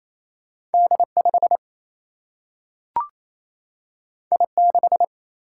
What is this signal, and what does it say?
Morse code, heard as a keyed tone, about 28 wpm, 700 Hz. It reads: D5 I6
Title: Single-Letter Single-Number 28wpm